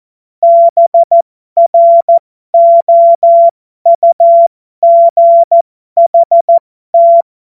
Morse code, heard as a keyed tone, about 14 wpm, spaced as for 12 wpm, 680 Hz. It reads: BROUGHT
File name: Morsekurs / Witze